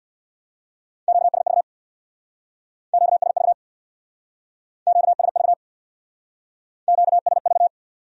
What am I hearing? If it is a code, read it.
Morse code, heard as a keyed tone, about 67 wpm, 700 Hz. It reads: QSV QSV QSV QSV